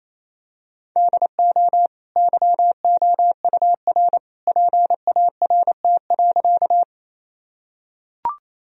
Morse code, heard as a keyed tone, about 28 wpm, 700 Hz. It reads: DO YOUR PART.
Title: Sentences from Top 1000 Words - Review Part 1 - 28wpm